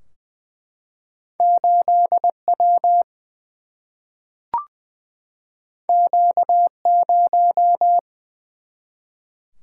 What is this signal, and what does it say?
Morse code, heard as a keyed tone, about 20 wpm, 700 Hz. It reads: 8 W Q 0
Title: Single-Letter Single-Number 20wpm